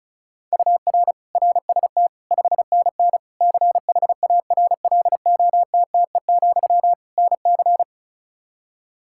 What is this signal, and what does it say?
Morse code, heard as a keyed tone, about 34 wpm, 700 Hz. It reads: UR RST 5NN CHARLOTTE, NC
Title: K1USN SST QSO Practice 35wpm